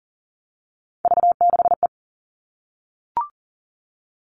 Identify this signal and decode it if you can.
Morse code, heard as a keyed tone, about 40 wpm, 700 Hz. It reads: V6E
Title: All Cut Numbers - 3-Digit 40wpm